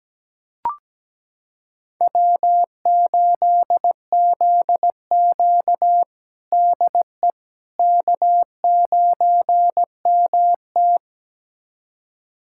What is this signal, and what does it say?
Morse code, heard as a keyed tone, about 17 wpm, 700 Hz. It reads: W8ZQ DE K9MT
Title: USA Callsign DE Callsign - 4 Characters - 17wpm